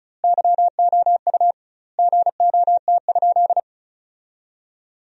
Morse code, heard as a keyed tone, about 34 wpm, 700 Hz. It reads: YOU GOT?
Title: 2x Word-Spacing - Sentences from Top 300 Words - 35wpm